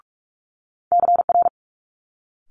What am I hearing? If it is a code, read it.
Morse code, heard as a keyed tone, about 46 wpm, 700 Hz. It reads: CR
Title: Multi-letter Phonemes Quick Fire 45wpm